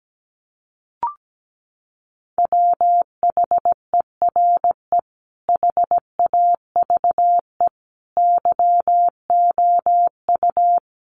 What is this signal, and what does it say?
Morse code, heard as a keyed tone, about 17 wpm, 700 Hz. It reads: WHERE HAVE YOU
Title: Sentences from Top 300 Words 17wpm